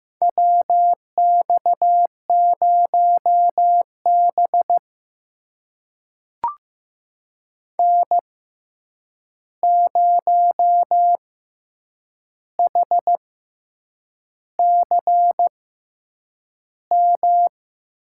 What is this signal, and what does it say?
Morse code, heard as a keyed tone, about 15 wpm, spaced as for 7 wpm, 700 Hz. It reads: WX0B N0HCM
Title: USA Call Signs - 6x Character Spacing - 15wpm